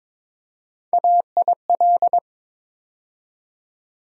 Morse code, heard as a keyed tone, about 22 wpm, 700 Hz. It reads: AIL
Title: USA Call Sign Postfixes - 3 Letters 22wpm